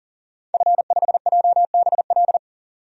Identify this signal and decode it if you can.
Morse code, heard as a keyed tone, about 40 wpm, 700 Hz. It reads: F5JBL
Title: DXCC-100 Call Signs - 5 Characters - 40wpm